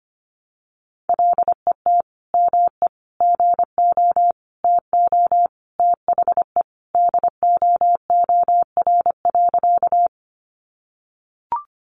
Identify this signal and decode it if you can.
Morse code, heard as a keyed tone, about 25 wpm, 700 Hz. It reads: LET ME GO TO THE DOOR.
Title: Sentences from Top 300 Words 25wpm